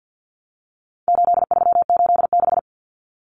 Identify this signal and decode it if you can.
Morse code, heard as a keyed tone, about 50 wpm, 700 Hz. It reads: 8386
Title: Numbers - 4-Digit 50wpm